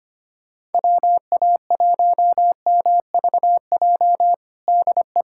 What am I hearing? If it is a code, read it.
Morse code, heard as a keyed tone, about 25 wpm, 700 Hz. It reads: WA1MVJ DE
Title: USA Callsign DE Callsign - 6 Characters - 25wpm